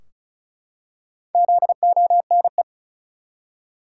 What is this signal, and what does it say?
Morse code, heard as a keyed tone, about 34 wpm, 700 Hz. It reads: ZONE
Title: Four Letter Words - 35wpm